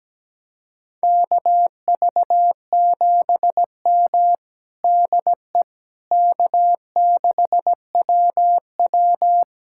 Morse code, heard as a keyed tone, about 17 wpm, 700 Hz. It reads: KV7M DE K6WW
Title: USA Callsign DE Callsign - 4 Characters - 17wpm